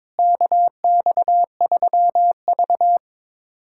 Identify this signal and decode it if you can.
Morse code, heard as a keyed tone, about 21 wpm, 700 Hz. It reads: KX3V
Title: USA Callsign DE Callsign - 4 Characters - 22wpm